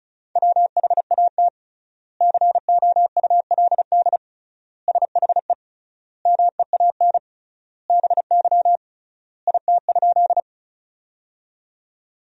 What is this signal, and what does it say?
Morse code, heard as a keyed tone, about 34 wpm, 700 Hz. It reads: WHAT COULD SHE MEAN BY IT?
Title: 3x Word-Spacing - Sentences from Top 200 Words - 35wpm